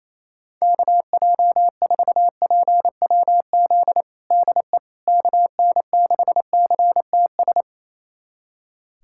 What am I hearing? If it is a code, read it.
Morse code, heard as a keyed tone, about 28 wpm, 700 Hz. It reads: KJ4PWZ DE KN6CTS
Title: USA Callsign DE Callsign 28wpm